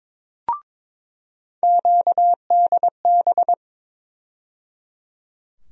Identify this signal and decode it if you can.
Morse code, heard as a keyed tone, about 22 wpm, 700 Hz. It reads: QDB
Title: USA Call Sign Postfixes - 3 Letters 22wpm